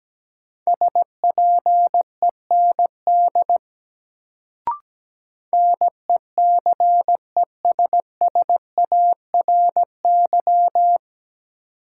Morse code, heard as a keyed tone, about 17 wpm, 700 Hz. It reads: SPEND NECESSARY
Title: Sentences from Top 800 Words - New Word Drill - 17wpm